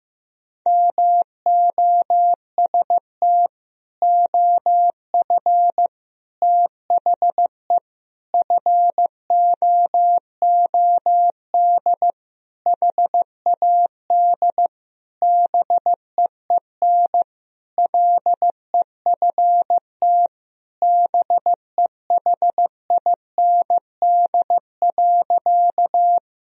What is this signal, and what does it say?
Morse code, heard as a keyed tone, about 15 wpm, 700 Hz. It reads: MOST OF THE FOOD HAD BEEN LEFT BEHIND.
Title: Sentences from Top 1000 Words - Review Part 1 - 15wpm